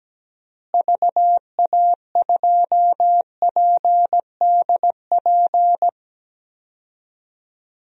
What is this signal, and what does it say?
Morse code, heard as a keyed tone, about 17 wpm, 700 Hz. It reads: VA2PDP
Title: DXCC-100 Call Signs - 6 Characters - Repeated 2x - 17wpm